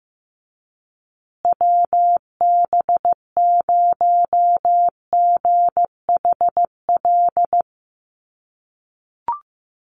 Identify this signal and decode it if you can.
Morse code, heard as a keyed tone, about 15 wpm, 700 Hz. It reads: WB0GHL
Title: USA Call Signs Repeated 2x 15wpm